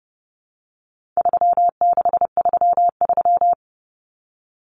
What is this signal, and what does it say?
Morse code, heard as a keyed tone, about 30 wpm, 700 Hz. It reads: 3633
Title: Common Cut Numbers - 4-Digit 30wpm